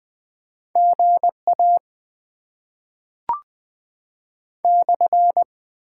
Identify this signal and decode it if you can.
Morse code, heard as a keyed tone, about 20 wpm, 700 Hz. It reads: GA /
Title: Top 90 QSO Elements - 8x Character Spacing - 20wpm